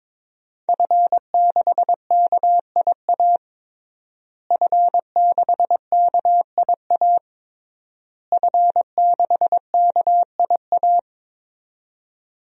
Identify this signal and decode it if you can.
Morse code, heard as a keyed tone, about 22 wpm, 700 Hz. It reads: F6KIA F6KIA F6KIA
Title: DXCC-100 Call Signs - 5 Characters - Repeated 3x - 22wpm